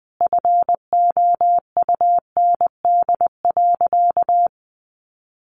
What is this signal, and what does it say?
Morse code, heard as a keyed tone, about 20 wpm, 700 Hz. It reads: FOUND.
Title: Sentences from Top 1300 Words 20wpm